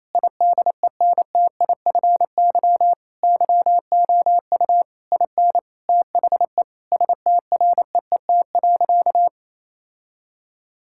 Morse code, heard as a keyed tone, about 28 wpm, 700 Hz. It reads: IDENTIFY YOU IN THE STREET.
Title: Sentences from Top 1000 Words - Review Part 6 - 28wpm